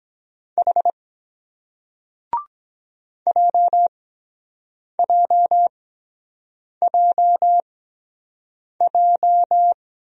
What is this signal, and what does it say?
Morse code, heard as a keyed tone, about 24 wpm, spaced as for 7 wpm, 700 Hz. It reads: H JJJJ